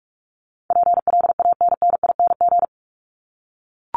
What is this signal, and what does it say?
Morse code, heard as a keyed tone, about 44 wpm, 700 Hz. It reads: PLANNING
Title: ING Suffix - Top 100 Words with Base Words 45wpm